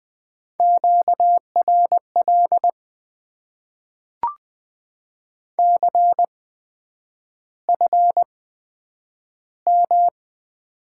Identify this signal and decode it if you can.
Morse code, heard as a keyed tone, about 20 wpm, spaced as for 9 wpm, 700 Hz. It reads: QRL CFM
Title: Top 90 QSO Elements - 8x Character Spacing - 20wpm